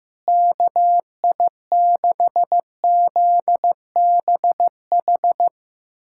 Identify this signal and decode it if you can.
Morse code, heard as a keyed tone, about 16 wpm, 700 Hz. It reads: KI6ZBH
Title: USA Callsign DE Callsign 15wpm